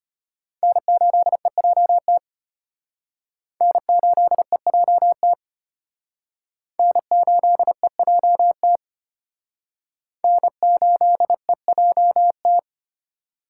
Morse code, heard as a keyed tone, about 34 wpm, 700 Hz. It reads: N8EJT N8EJT N8EJT N8EJT